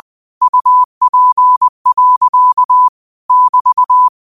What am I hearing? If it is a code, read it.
Morse code, heard as a keyed tone, about 20 wpm, 1000 Hz. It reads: UP. <BT>
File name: 20wpm